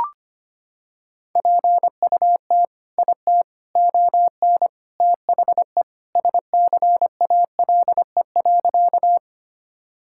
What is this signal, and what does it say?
Morse code, heard as a keyed tone, about 25 wpm, 700 Hz. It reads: PUT IT ON THE SCALE.
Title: Sentences from Top 1000 Words - Review Part 5 - 25wpm